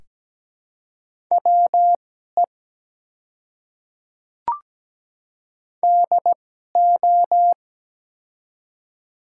Morse code, heard as a keyed tone, about 17 wpm, spaced as for 12 wpm, 700 Hz. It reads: WE DO
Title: 2x Character Spacing - Top 100 Words - 17wpm